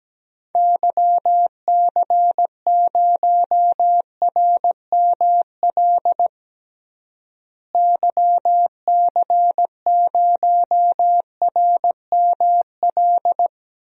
Morse code, heard as a keyed tone, about 17 wpm, 700 Hz. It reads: YC0RML YC0RML
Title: DXCC-100 Call Signs - 6 Characters - Repeated 3x - 17wpm